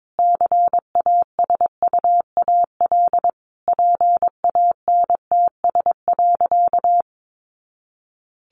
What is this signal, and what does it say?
Morse code, heard as a keyed tone, about 22 wpm, 700 Hz. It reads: CASUAL PANTS.
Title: Additional Easy Phrases 22wpm